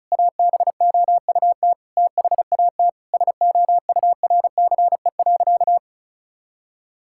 Morse code, heard as a keyed tone, about 36 wpm, 700 Hz. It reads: ABOUT THAT SOURCE.
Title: Sentences from Top 1000 Words - Review Part 5 - 35wpm